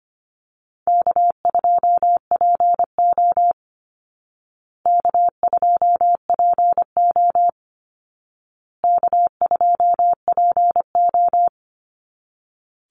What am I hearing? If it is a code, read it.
Morse code, heard as a keyed tone, about 25 wpm, 700 Hz. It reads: K2PO K2PO K2PO
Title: Easy Call Signs Repeated 3x - 25wpm